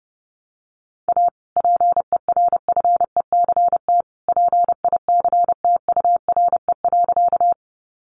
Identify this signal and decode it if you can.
Morse code, heard as a keyed tone, about 30 wpm, 700 Hz. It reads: A PERFECT PICTURE.
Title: Sentences from Top 1000 Words 30wpm